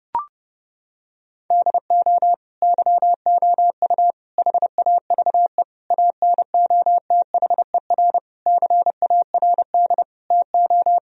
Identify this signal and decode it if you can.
Morse code, heard as a keyed tone, about 30 wpm, 700 Hz. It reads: DO YOU HAVE ANOTHER CARD TO
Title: Sentences from Top 1000 Words - Review Part 4 - 30wpm